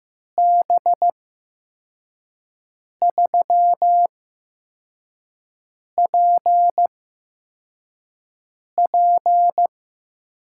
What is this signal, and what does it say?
Morse code, heard as a keyed tone, about 15 wpm, spaced as for 4 wpm, 700 Hz. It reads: B3PP
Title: USA Call Signs - 6 Characters - 8x Character Spacing - 15wpm